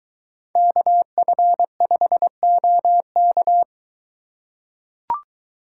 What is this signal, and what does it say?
Morse code, heard as a keyed tone, about 23 wpm, 700 Hz. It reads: KF5OK